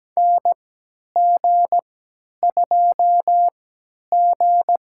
Morse code, heard as a keyed tone, about 17 wpm, spaced as for 10 wpm, 700 Hz. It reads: NG2G
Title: USA Call Signs - 4 Characters - 3x Character Spacing - 17wpm